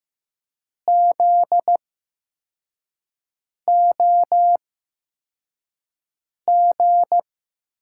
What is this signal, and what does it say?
Morse code, heard as a keyed tone, about 15 wpm, spaced as for 4 wpm, 700 Hz. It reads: ZOG
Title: USA Call Signs - 5 Characters - 8x Character Spacing - 15wpm